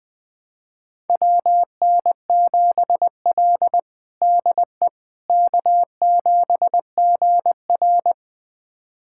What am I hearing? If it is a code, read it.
Morse code, heard as a keyed tone, about 20 wpm, 700 Hz. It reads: WN7L DE K7GR
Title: USA Callsign DE Callsign - 4 Characters - 20wpm